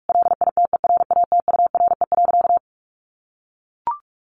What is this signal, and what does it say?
Morse code, heard as a keyed tone, about 44 wpm, 700 Hz. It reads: LITERATURE.
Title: Sentences from Top 900 Words 45wpm